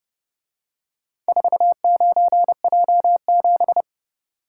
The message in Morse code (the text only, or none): 49J7